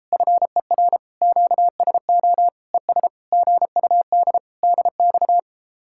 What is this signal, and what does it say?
Morse code, heard as a keyed tone, about 34 wpm, 700 Hz. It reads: FER QSO ES GUD DX